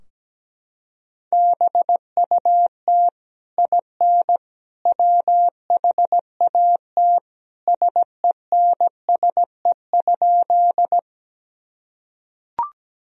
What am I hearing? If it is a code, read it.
Morse code, heard as a keyed tone, about 17 wpm, 700 Hz. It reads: BUT IN WHAT SENSE?
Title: Sentences from Top 1000 Words - Review Part 4 - 17wpm